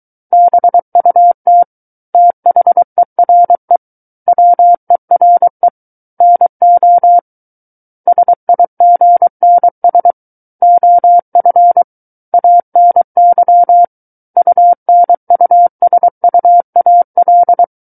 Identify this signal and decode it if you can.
Morse code, heard as a keyed tone, about 23 wpm, 700 Hz. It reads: BUT THERE WERE NO SIGNS OF ANY UNUSUAL
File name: Book One / 14-Chapter 14 - 23 WPM